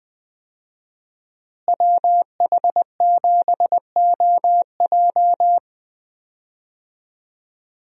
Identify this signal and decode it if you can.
Morse code, heard as a keyed tone, about 20 wpm, 700 Hz. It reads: WH7OJ
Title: USA Call Signs - 5 Characters - Repeated 3x - 20wpm